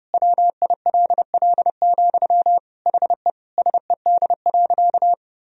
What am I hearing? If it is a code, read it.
Morse code, heard as a keyed tone, about 30 wpm, 700 Hz. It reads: WILL, HE SED.
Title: Sentences from Top 200 Words and Abbreviations 30wpm